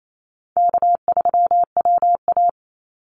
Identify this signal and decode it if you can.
Morse code, heard as a keyed tone, about 28 wpm, 700 Hz. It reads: K3WA
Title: Easy Call Signs Repeated 3x - 28wpm